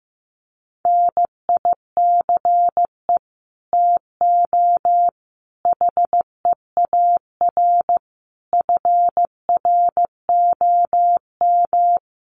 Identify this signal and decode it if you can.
Morse code, heard as a keyed tone, about 15 wpm, 700 Hz. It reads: NICE TO HEAR FROM
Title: Common QSO Phrases 15wpm